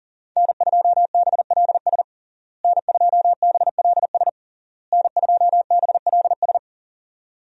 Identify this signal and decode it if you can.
Morse code, heard as a keyed tone, about 40 wpm, 700 Hz. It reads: N2BLS N2BLS N2BLS
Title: USA Call Signs - 5 Characters - Repeated 3x - 40wpm